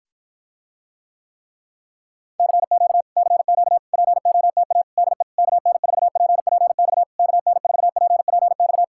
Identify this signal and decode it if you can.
Morse code, heard as a keyed tone, about 60 wpm, 700 Hz. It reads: CQ CQ POTA DE KN4PPX KN4PPX